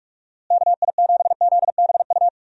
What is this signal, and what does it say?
Morse code, heard as a keyed tone, about 46 wpm, 700 Hz. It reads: KI7ZBU
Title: USA Call Signs - 6 Characters - Repeated 3x - 45wpm